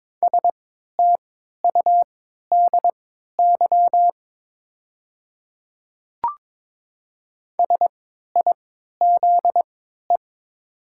Morse code, heard as a keyed tone, about 22 wpm, spaced as for 13 wpm, 700 Hz. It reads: STUDY SIZE
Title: 3x Character Spacing - Top 500 Words - 22wpm